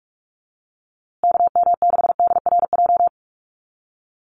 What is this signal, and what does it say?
Morse code, heard as a keyed tone, about 44 wpm, 700 Hz. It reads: KM6DRJ
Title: USA Call Signs Repeated 3x 45wpm